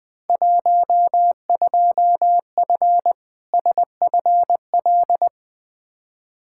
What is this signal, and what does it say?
Morse code, heard as a keyed tone, about 20 wpm, 700 Hz. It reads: 12F SFL
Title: Field Day Exchanges Encore - 20wpm